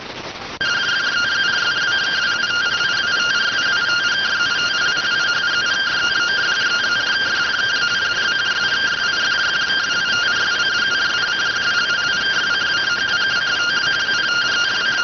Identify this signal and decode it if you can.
Two-tone teleprinter signal, about 170 Hz shift.